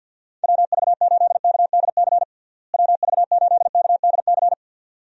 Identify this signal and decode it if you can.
Morse code, heard as a keyed tone, about 50 wpm, 700 Hz. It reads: WV8KDC WV8KDC
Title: USA Call Signs - 6 Characters - Repeated 2x - 50wpm